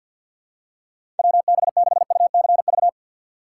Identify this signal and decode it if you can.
Morse code, heard as a keyed tone, about 48 wpm, 690 Hz. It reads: WB6UCV
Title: USA Call Signs - 6 Characters - 3x Character Spacing - 50wpm